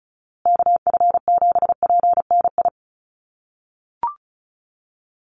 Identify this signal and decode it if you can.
Morse code, heard as a keyed tone, about 34 wpm, 700 Hz. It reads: KF7PNI
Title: USA Call Signs Repeated 3x 35wpm